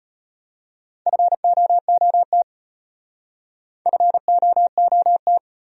Morse code, heard as a keyed tone, about 34 wpm, 700 Hz. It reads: FOOT FOOT